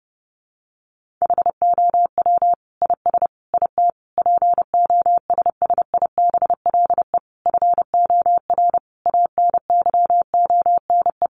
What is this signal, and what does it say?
Morse code, heard as a keyed tone, about 30 wpm, 700 Hz. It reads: HOW IS IT POSSIBLE FOR ANYONE
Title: Sentences from Top 900 Words 30wpm